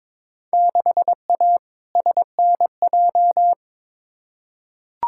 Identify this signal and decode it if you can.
Morse code, heard as a keyed tone, about 21 wpm, 700 Hz. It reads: 6A SNJ
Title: Field Day Exchanges Encore - 22wpm